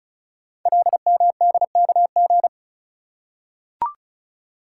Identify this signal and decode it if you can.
Morse code, heard as a keyed tone, about 36 wpm, 700 Hz. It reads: LMDKG
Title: Code Groups - 5 Letters - 35wpm